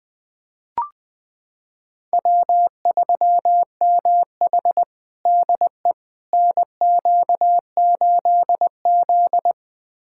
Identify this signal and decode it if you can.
Morse code, heard as a keyed tone, about 20 wpm, 700 Hz. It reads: W3MH DE NQ8Z
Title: USA Callsign DE Callsign - 4 Characters - 20wpm